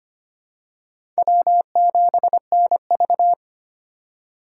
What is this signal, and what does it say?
Morse code, heard as a keyed tone, about 25 wpm, 700 Hz. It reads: W7NV
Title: USA Call Signs - 3x Character Spacing - 25wpm